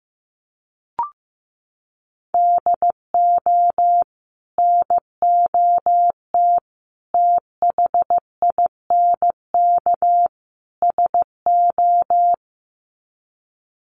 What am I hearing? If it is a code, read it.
Morse code, heard as a keyed tone, about 15 wpm, 700 Hz. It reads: DO NOT THINK SO
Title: Sets of 4 Words 15wpm